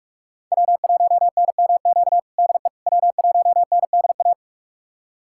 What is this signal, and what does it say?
Morse code, heard as a keyed tone, about 46 wpm, 700 Hz. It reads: W1NMQ DE W1NDA
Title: USA Callsign DE Callsign 45wpm